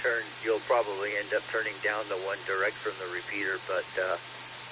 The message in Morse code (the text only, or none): none